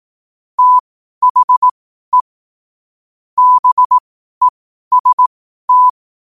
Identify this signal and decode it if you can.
Morse code, heard as a keyed tone, about 18 wpm, spaced as for 12 wpm, 1000 Hz. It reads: THE BEST
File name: qotdfiles / 12wpm